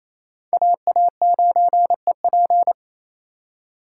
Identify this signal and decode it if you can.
Morse code, heard as a keyed tone, about 28 wpm, 700 Hz. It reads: AA9EP
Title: USA Call Signs - 5 Characters - 2x Character Spacing - 28wpm